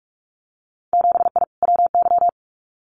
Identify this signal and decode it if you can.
Morse code, heard as a keyed tone, about 46 wpm, 700 Hz. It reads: 7I WY
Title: Winter Field Day Exchanges Encore 45wpm